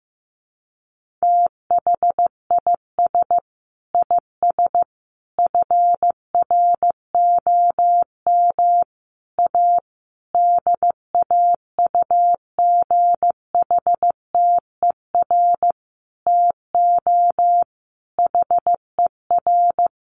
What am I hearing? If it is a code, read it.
Morse code, heard as a keyed tone, about 15 wpm, 700 Hz. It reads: THIS IS FROM A DAUGHTER TO HER
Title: Sentences from Top 1500 Words 15wpm